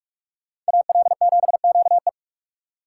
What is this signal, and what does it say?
Morse code, heard as a keyed tone, about 55 wpm, 700 Hz. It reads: AL7QE
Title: Speed Racing - USA 2x2 Call Signs 45wpm